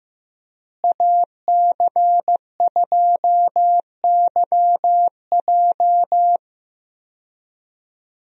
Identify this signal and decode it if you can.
Morse code, heard as a keyed tone, about 15 wpm, 700 Hz. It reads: AC2YJ
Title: USA Call Signs - 5 Characters - Repeated 2x - 15wpm